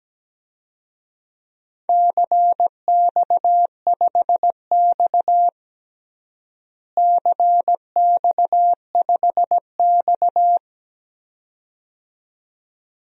Morse code, heard as a keyed tone, about 17 wpm, 700 Hz. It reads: CX5X CX5X
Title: DXCC-100 Call Signs - 4 Characters - Repeated 2x - 17wpm